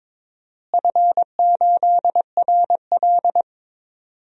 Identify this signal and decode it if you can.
Morse code, heard as a keyed tone, about 21 wpm, 700 Hz. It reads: F8RL
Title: DXCC-100 Call Signs - 4 Characters - Repeated 2x - 22wpm